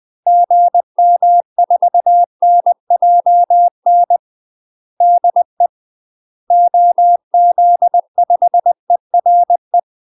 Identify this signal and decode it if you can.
Morse code, heard as a keyed tone, about 20 wpm, 700 Hz. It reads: GM4NJN DE OZ5ERE
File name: UK MASTER QSOs TRAINING / 20WPM / UK MASTER QSOs 0047